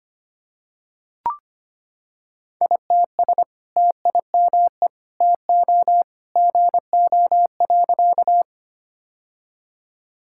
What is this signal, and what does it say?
Morse code, heard as a keyed tone, about 25 wpm, 700 Hz. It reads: ITS TIME TO GO.
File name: practice / 25wpm / Sentences with 2-Word Phrases - 25wpm